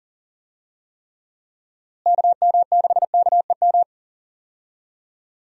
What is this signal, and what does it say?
Morse code, heard as a keyed tone, about 40 wpm, 700 Hz. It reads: KM6KEM
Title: USA Call Signs - 6 Characters - Repeated 2x - 40wpm